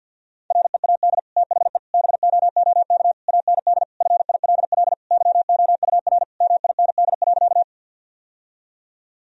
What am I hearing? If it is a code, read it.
Morse code, heard as a keyed tone, about 50 wpm, 700 Hz. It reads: READ THE BOOK AND FILL YOUR MIND.
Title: Sentences from Top 1000 Words - Review Part 3 - 50wpm